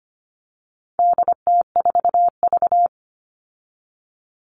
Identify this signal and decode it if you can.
Morse code, heard as a keyed tone, about 25 wpm, 700 Hz. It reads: DT4V
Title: All Cut Numbers - 4-Digit 25wpm